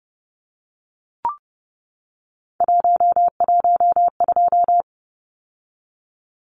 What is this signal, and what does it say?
Morse code, heard as a keyed tone, about 30 wpm, 700 Hz. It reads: 112
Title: Numbers - 3-Digit 30wpm